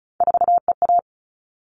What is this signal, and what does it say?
Morse code, heard as a keyed tone, about 34 wpm, 700 Hz. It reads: 4EA
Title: All Cut Numbers - 3-Digit 35wpm